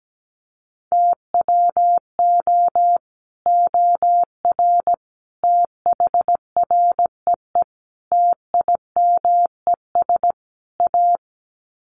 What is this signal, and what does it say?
Morse code, heard as a keyed tone, about 17 wpm, 700 Hz. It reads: TWO OR THREE TIMES A
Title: Sets of 5 Words Encore2 17wpm